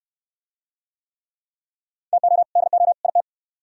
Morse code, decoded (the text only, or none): TO DO IT